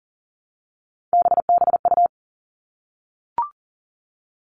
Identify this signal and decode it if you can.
Morse code, heard as a keyed tone, about 40 wpm, 700 Hz. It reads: BBU
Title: All Cut Numbers - 3-Digit 40wpm